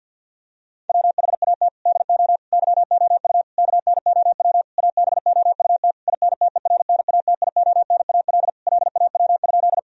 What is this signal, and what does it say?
Morse code, heard as a keyed tone, about 50 wpm, 700 Hz. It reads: WHAT DO YOU KNOW ABOUT INTERNATIONAL LAW?
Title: Sentences from Top 1000 Words - Review Part 4 - 50wpm